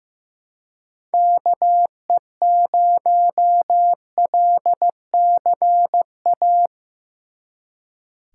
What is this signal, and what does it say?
Morse code, heard as a keyed tone, about 18 wpm, 700 Hz. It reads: KE0LCA
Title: Speed Racing - USA Call Signs 15wpm